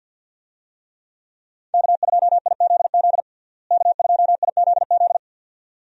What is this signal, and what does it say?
Morse code, heard as a keyed tone, about 52 wpm, 700 Hz. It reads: K2IZZ K2IZZ
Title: USA Call Signs - 5 Characters - Repeated 2x - 50wpm